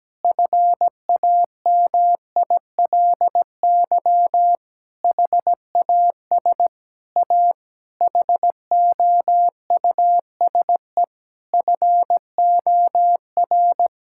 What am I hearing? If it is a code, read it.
Morse code, heard as a keyed tone, about 17 wpm, 700 Hz. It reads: FAMILY HAS A HOUSE FOR
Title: Sentences from Top 1000 Words - Review Part 5 - 17wpm